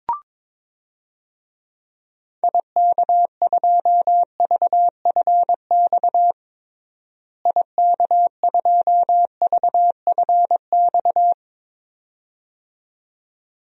Morse code, heard as a keyed tone, about 22 wpm, 700 Hz. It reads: IK2VFX IK2VFX
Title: DXCC-100 Call Signs - 6 Characters - Repeated 2x - 22wpm